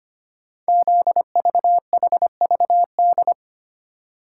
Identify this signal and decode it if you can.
Morse code, heard as a keyed tone, about 25 wpm, 700 Hz. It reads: ZVHVD
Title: Code Groups - 5 Letters - 25wpm